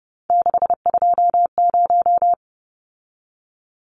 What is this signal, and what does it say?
Morse code, heard as a keyed tone, about 30 wpm, 700 Hz. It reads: 620
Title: Numbers - 3-Digit 30wpm